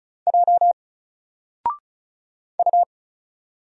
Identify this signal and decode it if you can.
Morse code, heard as a keyed tone, about 36 wpm, 700 Hz. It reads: J U
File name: Single-Letter-Number-35wpm